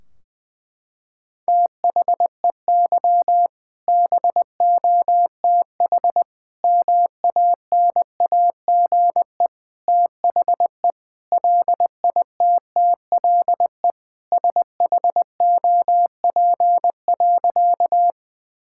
Morse code, heard as a keyed tone, about 20 wpm, 700 Hz. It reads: THEY BOTH MANAGE THE LITTLE SHOP.
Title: Sentences from Top 1600 Words 20wpm